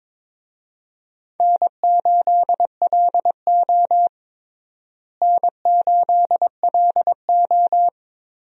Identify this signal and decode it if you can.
Morse code, heard as a keyed tone, about 21 wpm, 700 Hz. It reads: N8LO N8LO
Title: USA Call Signs - 4 Characters - Repeated 2x - 22wpm